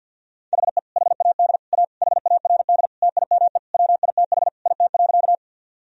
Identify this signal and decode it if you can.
Morse code, heard as a keyed tone, about 47 wpm, 700 Hz. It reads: HE HAD A HARD TIME WITH IT.
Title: Sentences from Top 1000 Words - Review Part 1 - 50wpm